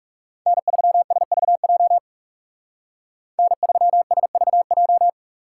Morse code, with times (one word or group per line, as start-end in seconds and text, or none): 0.5-2.0 N3SVJ
3.4-5.1 N3SVJ